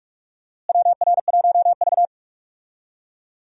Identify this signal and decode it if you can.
Morse code, heard as a keyed tone, about 44 wpm, 700 Hz. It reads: WR1V
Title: USA Call Signs - 4 Characters - Repeated 2x - 45wpm